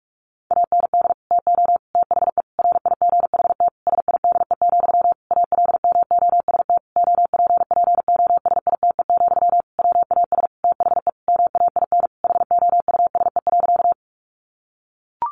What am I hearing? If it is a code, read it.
Morse code, heard as a keyed tone, about 44 wpm, 700 Hz. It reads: AND TO THE RIGHT SIDE, ALMOST OPPOSITE, WAS THE MAIN HOUSE.
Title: Sentences from Top 1300 Words 45wpm